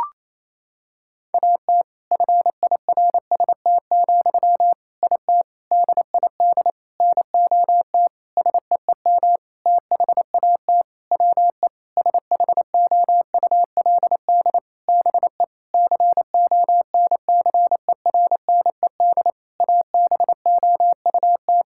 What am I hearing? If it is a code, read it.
Morse code, heard as a keyed tone, about 28 wpm, 700 Hz. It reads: AT FIRST, IT DID NOT SEEM THAT WE SHOULD BE CONCERNED ABOUT
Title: Sentences from Top 1000 Words - Review Part 6 - 28wpm